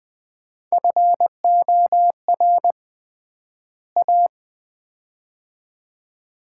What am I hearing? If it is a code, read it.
Morse code, heard as a keyed tone, about 20 wpm, 700 Hz. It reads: FOR A
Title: 3x Word-Spacing - Sets of 2 Words - 20wpm